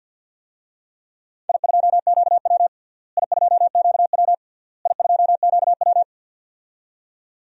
Morse code, heard as a keyed tone, about 47 wpm, 700 Hz. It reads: I2QW I2QW I2QW
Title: DXCC-100 Call Signs - 4 Characters - Repeated 3x - 50wpm